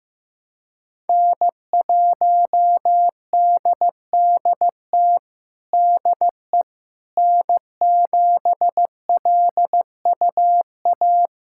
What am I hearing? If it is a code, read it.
Morse code, heard as a keyed tone, about 15 wpm, 700 Hz. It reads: N1DDT DE N7LUA
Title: USA Callsign DE Callsign - 5 Characters - 15wpm